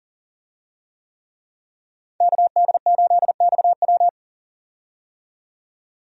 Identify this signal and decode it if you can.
Morse code, heard as a keyed tone, about 40 wpm, 700 Hz. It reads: KD8XW
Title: USA Call Signs - 5 Characters - 40wpm